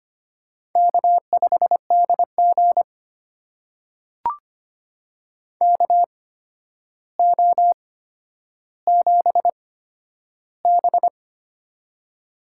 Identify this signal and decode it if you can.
Morse code, heard as a keyed tone, about 25 wpm, spaced as for 11 wpm, 700 Hz. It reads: K5DG KO7B
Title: USA Call Signs - 4 Characters - 8x Character Spacing - 25wpm